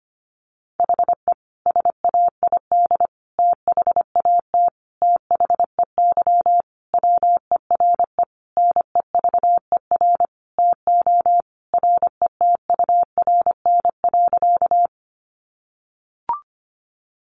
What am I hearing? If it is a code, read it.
Morse code, heard as a keyed tone, about 25 wpm, 700 Hz. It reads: HE SAID THAT THEY WERE NEVER TO RETURN.
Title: Sentences from Top 800 Words 25wpm